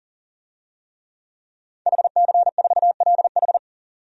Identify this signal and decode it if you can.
Morse code, heard as a keyed tone, about 40 wpm, 700 Hz. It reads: HC4LH
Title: DXCC-100 Call Signs - 5 Characters - 40wpm